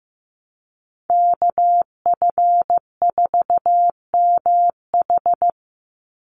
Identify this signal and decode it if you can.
Morse code, heard as a keyed tone, about 15 wpm, 700 Hz. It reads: KF4MH
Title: NA SOTA Chaser Callsigns 15wpm